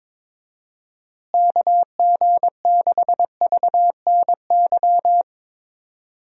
Morse code, heard as a keyed tone, about 22 wpm, 700 Hz. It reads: KG6VNY
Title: USA Call Signs - 6 Characters - Repeated 3x - 22wpm